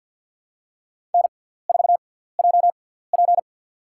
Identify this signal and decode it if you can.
Morse code, heard as a keyed tone, about 53 wpm, spaced as for 17 wpm, 700 Hz. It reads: N4JP
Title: USA Call Signs - 6x Character Spacing - 50wpm